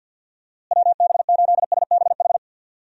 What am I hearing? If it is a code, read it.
Morse code, heard as a keyed tone, about 52 wpm, 700 Hz. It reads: WB8SBH
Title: USA Call Signs - 6 Characters - Repeated 3x - 50wpm